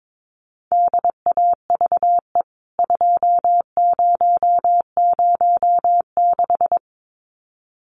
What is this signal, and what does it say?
Morse code, heard as a keyed tone, about 22 wpm, 700 Hz. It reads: DAVE 2006
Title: CWT Exchange 22wpm